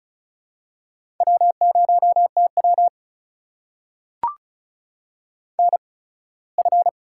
Morse code, heard as a keyed tone, about 36 wpm, spaced as for 21 wpm, 700 Hz. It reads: W0TW NF
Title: USA Call Signs - 4 Characters - 8x Character Spacing - 35wpm